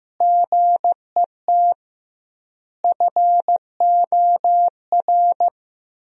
Morse code, heard as a keyed tone, about 16 wpm, 700 Hz. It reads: GET FOR
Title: 2x Word-Spacing - Sentences from Top 400 Words - 15wpm